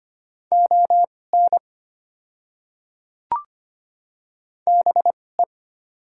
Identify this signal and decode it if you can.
Morse code, heard as a keyed tone, about 25 wpm, spaced as for 18 wpm, 690 Hz. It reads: ON BE
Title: 2x Character Spacing - Top 100 Words - 25wpm